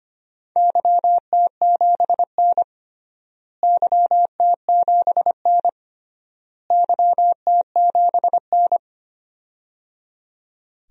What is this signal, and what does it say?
Morse code, heard as a keyed tone, about 25 wpm, 700 Hz. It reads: YT7N YT7N YT7N
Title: DXCC-100 Call Signs - 4 Characters - Repeated 3x - 25wpm